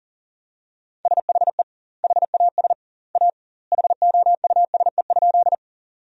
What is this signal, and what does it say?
Morse code, heard as a keyed tone, about 40 wpm, 700 Hz. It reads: SHE HAS A HOUSE?
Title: 2x Word-Spacing - Sentences from Top 200 Words - 40wpm